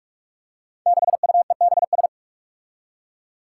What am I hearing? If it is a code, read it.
Morse code, heard as a keyed tone, about 46 wpm, 700 Hz. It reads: 6UEBS
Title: Five Challenging Letter-Number Combinations 45wpm